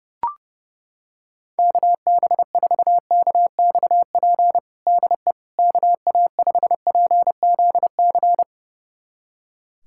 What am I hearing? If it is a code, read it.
Morse code, heard as a keyed tone, about 30 wpm, 700 Hz. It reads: KB4KXP DE KA5PZC
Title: USA Callsign DE Callsign - 6 Characters - 30wpm